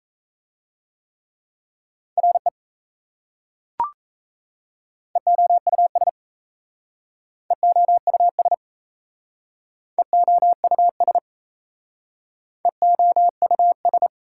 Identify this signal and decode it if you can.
Morse code, heard as a keyed tone, about 37 wpm, 700 Hz. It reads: RE EOUS EOUS EOUS EOUS